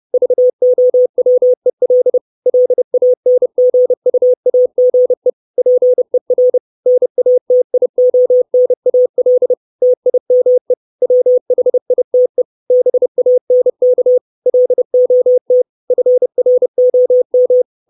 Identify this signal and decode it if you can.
Morse code, heard as a keyed tone, about 30 wpm, 500 Hz. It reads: VOWEL LANGUAGE PER NATIONAL TIME WHITE BANK LOT FROM
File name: 632 Words 30WPM